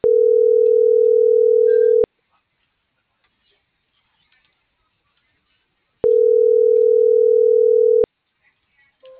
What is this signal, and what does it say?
Ringback tone (North America), 440 + 480 Hz.